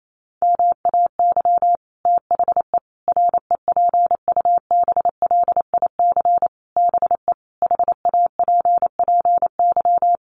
Morse code, heard as a keyed tone, about 28 wpm, 700 Hz. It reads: MAY THE REPUBLIC BE HAPPY
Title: Sentences from Top 1500 Words 28wpm